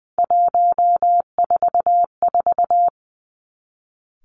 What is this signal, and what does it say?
Morse code, heard as a keyed tone, about 20 wpm, 700 Hz. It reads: 144
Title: Three Challenging Letter-Number Combinations 20wpm